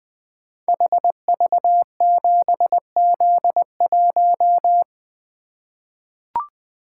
Morse code, heard as a keyed tone, about 20 wpm, 700 Hz. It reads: HV7Z1
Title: Code Groups - 5 Letter:Numbers - 20wpm